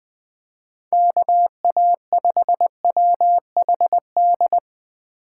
Morse code, heard as a keyed tone, about 20 wpm, 700 Hz. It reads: KA5WHD
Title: USA Call Signs - 6 Characters - 3x Character Spacing - 20wpm